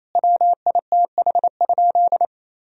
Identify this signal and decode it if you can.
Morse code, heard as a keyed tone, about 28 wpm, 700 Hz. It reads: WITH?
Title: Sentences from Top 1000 Words - Review Part 1 - 28wpm